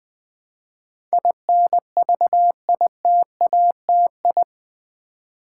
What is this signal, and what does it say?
Morse code, heard as a keyed tone, about 20 wpm, 700 Hz. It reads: INVITATI
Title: Build A Word - 4 Letters At A Time - 10-Letter Words - 20wpm